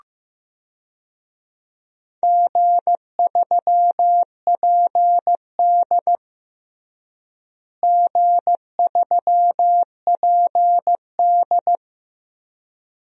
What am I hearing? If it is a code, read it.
Morse code, heard as a keyed tone, about 16 wpm, 700 Hz. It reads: G3PD G3PD
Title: DXCC-100 Call Signs - 4 Characters - Repeated 2x - 15wpm